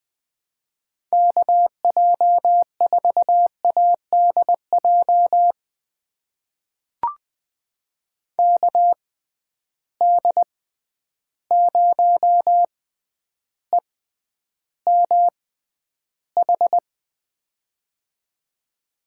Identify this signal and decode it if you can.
Morse code, heard as a keyed tone, about 21 wpm, spaced as for 10 wpm, 700 Hz. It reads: KJ4ADJ KD0EMH
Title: USA Call Signs - 6 Characters - 6x Character Spacing - 20wpm